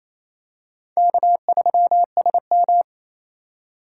Morse code, heard as a keyed tone, about 28 wpm, 700 Hz. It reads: K3SM
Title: USA Call Signs - 4 Characters - 8x Character Spacing - 28wpm